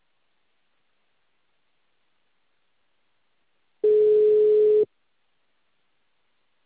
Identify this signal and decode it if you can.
Ringback tone (Europe, 425 Hz).